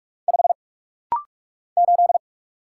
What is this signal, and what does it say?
Morse code, heard as a keyed tone, about 48 wpm, 700 Hz. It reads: 5 8
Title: Single Letter-Number - Rapid-Fire 45wpm